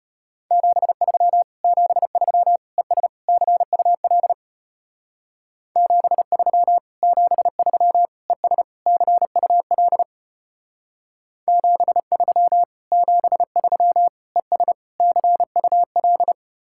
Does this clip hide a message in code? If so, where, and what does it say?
Yes: Morse code from 0.5 to 16.3 s, heard as a keyed tone, about 32 wpm, 700 Hz. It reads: 73 73 ES CUL 73 73 ES CUL 73 73 ES CUL